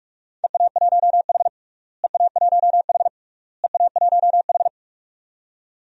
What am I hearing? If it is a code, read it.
Morse code, heard as a keyed tone, about 41 wpm, 700 Hz. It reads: EA1H EA1H EA1H
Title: DXCC-100 Call Signs - 4 Characters - Repeated 3x - 45wpm